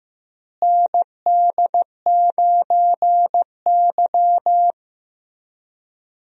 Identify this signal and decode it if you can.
Morse code, heard as a keyed tone, about 15 wpm, 700 Hz. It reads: ND9Y
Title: USA Call Signs - 4 Characters - Repeated 3x - 15wpm